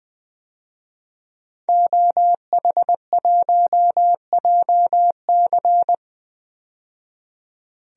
Morse code, heard as a keyed tone, about 20 wpm, 700 Hz. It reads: OH1JC
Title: DXCC-100 Call Signs - 5 Characters - 20wpm